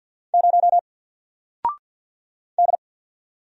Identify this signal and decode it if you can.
Morse code, heard as a keyed tone, about 50 wpm, 700 Hz. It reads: 0 D
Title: Single Letter-Number 50wpm